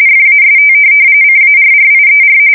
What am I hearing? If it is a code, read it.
RTTY, about 45.45 baud. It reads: FAST CARRIER